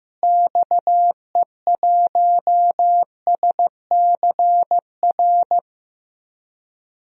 Morse code, heard as a keyed tone, about 15 wpm, 700 Hz. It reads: XE1SCR
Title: DXCC-100 Call Signs - 6 Characters - Repeated 3x - 15wpm